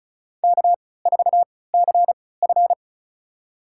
Morse code, heard as a keyed tone, about 34 wpm, spaced as for 20 wpm, 700 Hz. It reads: K4CF
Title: USA Call Signs - 4 Characters - 3x Character Spacing - 35wpm